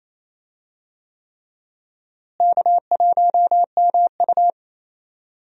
Morse code, heard as a keyed tone, about 28 wpm, 700 Hz. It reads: K1MU
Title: USA Call Signs - 4 Characters - Repeated 3x - 28wpm